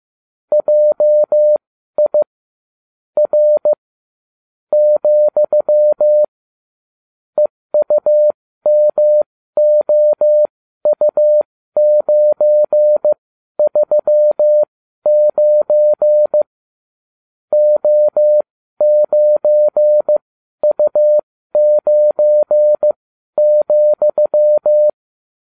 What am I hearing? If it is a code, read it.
Morse code, heard as a keyed tone, about 15 wpm, spaced as for 12 wpm, 600 Hz. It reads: JI R , EUMOU939 O9U9,